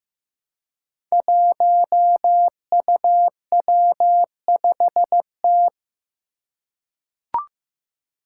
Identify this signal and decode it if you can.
Morse code, heard as a keyed tone, about 15 wpm, 700 Hz. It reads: 1UW5T
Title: Five Challenging Letter-Number Combinations 15wpm